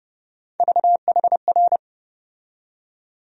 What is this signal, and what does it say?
Morse code, heard as a keyed tone, about 30 wpm, 700 Hz. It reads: VHR
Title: USA Call Sign Postfixes - 3 Letters 30wpm